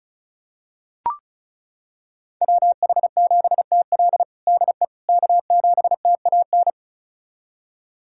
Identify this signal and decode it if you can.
Morse code, heard as a keyed tone, about 36 wpm, 700 Hz. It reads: WH7TL DE K7TAN
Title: USA Callsign DE Callsign - 5 Characters - 35wpm